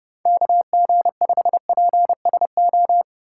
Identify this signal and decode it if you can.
Morse code, heard as a keyed tone, about 30 wpm, 700 Hz. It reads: KG5PSO
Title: USA Call Signs - 6 Characters - 8x Character Spacing - 30wpm